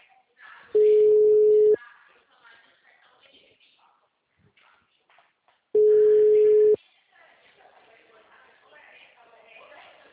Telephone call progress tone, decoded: ringback tone

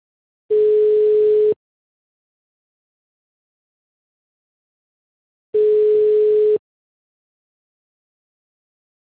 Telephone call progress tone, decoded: ringback tone

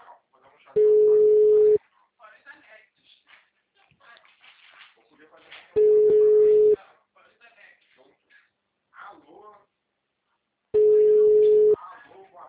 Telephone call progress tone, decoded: ringback tone

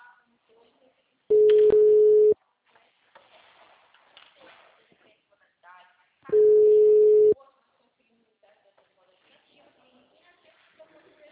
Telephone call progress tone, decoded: ringback tone

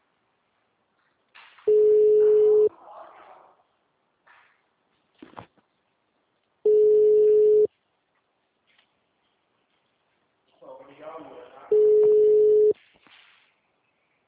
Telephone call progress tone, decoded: ringback tone

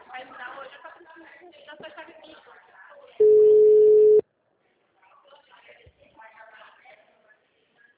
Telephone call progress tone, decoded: ringback tone

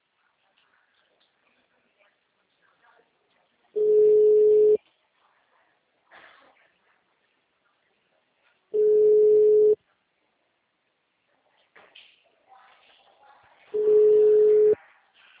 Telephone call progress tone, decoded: ringback tone

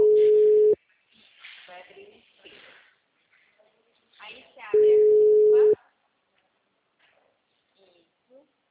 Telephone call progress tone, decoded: ringback tone

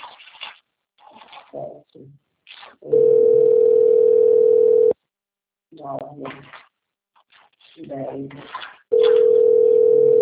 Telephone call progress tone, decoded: ringback tone

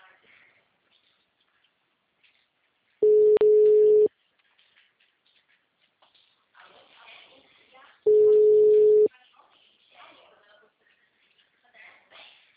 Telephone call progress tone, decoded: ringback tone

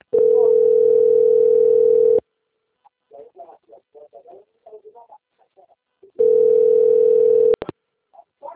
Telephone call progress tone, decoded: ringback tone

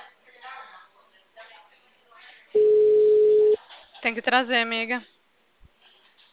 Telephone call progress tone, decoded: ringback tone

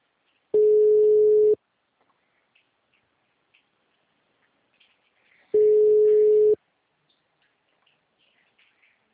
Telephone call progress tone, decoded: ringback tone

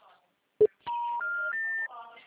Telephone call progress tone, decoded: special information tone

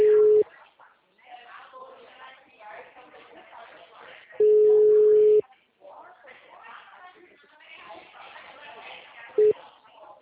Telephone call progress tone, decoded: ringback tone

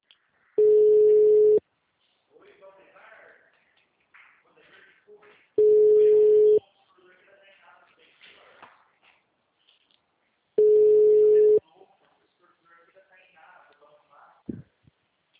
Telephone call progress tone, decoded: ringback tone

